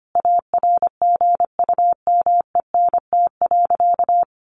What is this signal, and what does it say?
Morse code, heard as a keyed tone, about 25 wpm, 700 Hz. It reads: ARGUMENT.
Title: Sentences from Top 1400 Words 25wpm